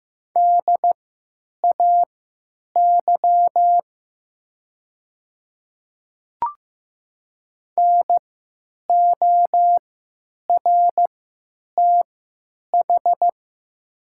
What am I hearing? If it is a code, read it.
Morse code, heard as a keyed tone, about 15 wpm, spaced as for 9 wpm, 700 Hz. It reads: DAY NORTH
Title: 3x Character Spacing - Top 300 Words - 15wpm